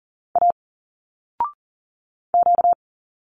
Morse code, heard as a keyed tone, about 40 wpm, 700 Hz. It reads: A Q
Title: Single Letters 40wpm